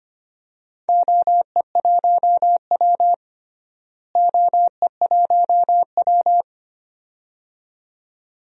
Morse code, heard as a keyed tone, about 25 wpm, 700 Hz. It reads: OE1W OE1W
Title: DXCC-100 Call Signs - 4 Characters - Repeated 2x - 25wpm